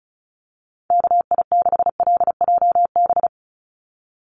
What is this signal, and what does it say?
Morse code, heard as a keyed tone, about 34 wpm, 700 Hz. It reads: KI6LJB
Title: USA Call Signs 35wpm